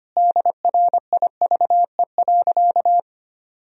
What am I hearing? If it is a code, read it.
Morse code, heard as a keyed tone, about 25 wpm, 700 Hz. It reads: DRIVE.
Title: Sentences from Top 1000 Words - Review Part 2 - 25wpm